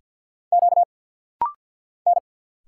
Morse code, heard as a keyed tone, about 49 wpm, 700 Hz. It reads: Q N
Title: Single Letter-Number Alt - Rapid-Fire 50wpm